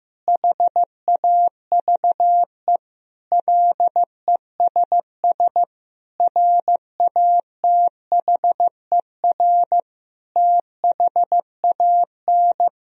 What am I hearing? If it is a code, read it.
Morse code, heard as a keyed tone, about 15 wpm, 700 Hz. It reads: HAVE LESS RATHER THAN
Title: Sentences from Top 1000 Words - Review Part 4 - 15wpm